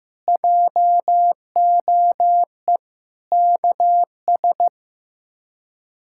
Morse code, heard as a keyed tone, about 15 wpm, 700 Hz. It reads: JOE KS
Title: K1USN SST QSO Practice - 15wpm